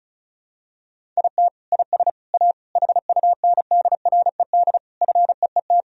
Morse code, heard as a keyed tone, about 34 wpm, 700 Hz. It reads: IT IS A HUNDRED FEET
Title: Sentences from Top 1000 Words - Review Part 2 - 35wpm